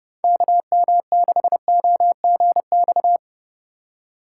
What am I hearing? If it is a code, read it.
Morse code, heard as a keyed tone, about 30 wpm, 700 Hz. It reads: KM6OGX
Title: USA Call Signs - 6 Characters - 4x Character Spacing - 30wpm